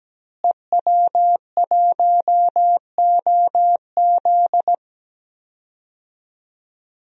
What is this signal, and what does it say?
Morse code, heard as a keyed tone, about 17 wpm, 700 Hz. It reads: EW1OZ
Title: DXCC-100 Call Signs - 5 Characters - Repeated 3x - 17wpm